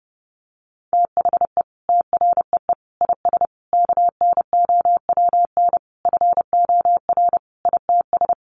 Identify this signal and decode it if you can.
Morse code, heard as a keyed tone, about 30 wpm, 700 Hz. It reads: THE TREE IS KNOWN FOR ITS
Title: Sentences from Top 1300 Words 30wpm